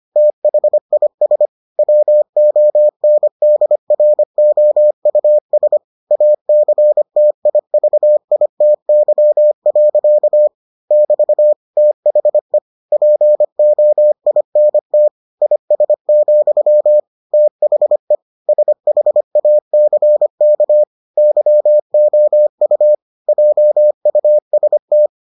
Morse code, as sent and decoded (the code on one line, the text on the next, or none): - .... .. ... / .-- --- -. -.. .-. --- ..- ... / .- -.-. - .. ...- .. - -.-- .-.-.- / -...- / - .... / .--. --- .. -. - / .. ... --..-- / - .... / ... .... .- -.-. -.- / -.-- --- ..- / .--- ..- ... -
THIS WONDROUS ACTIVITY. <BT> THE POINT IS, THE SHACK YOU JUST